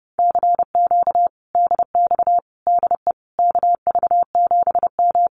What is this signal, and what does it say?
Morse code, heard as a keyed tone, about 32 wpm, 700 Hz. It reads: CQ DX DE KV7M
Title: Common QSO Phrases Encore 30wpm